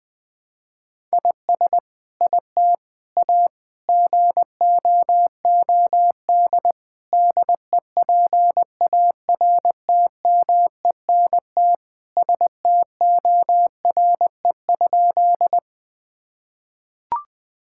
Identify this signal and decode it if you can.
Morse code, heard as a keyed tone, about 19 wpm, 700 Hz. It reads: IS IT A GOOD DEPARTMENT STORE?
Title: Sentences from Top 1000 Words - Review Part 5 - 20wpm